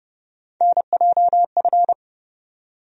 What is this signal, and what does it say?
Morse code, heard as a keyed tone, about 30 wpm, 700 Hz. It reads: NJF
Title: USA Call Sign Postfixes - 3 Letters 30wpm